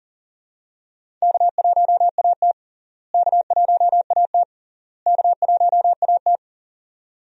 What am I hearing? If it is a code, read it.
Morse code, heard as a keyed tone, about 40 wpm, 700 Hz. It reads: K1AT K1AT K1AT
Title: USA Call Signs - 4 Characters - Repeated 3x - 40wpm